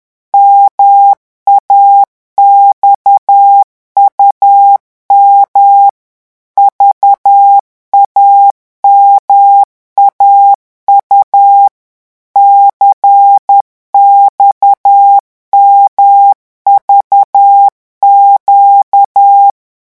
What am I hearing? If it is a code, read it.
Morse code, heard as a keyed tone, about 10 wpm, 800 Hz. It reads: MAXUM VAMAU CXMVQ
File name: morse mp3 / revision2-vitesse 10 mots